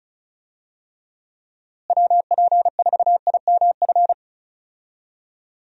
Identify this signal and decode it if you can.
Morse code, heard as a keyed tone, about 34 wpm, 700 Hz. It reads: WP4IMF
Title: USA Call Signs - 6 Characters - Repeated 2x - 35wpm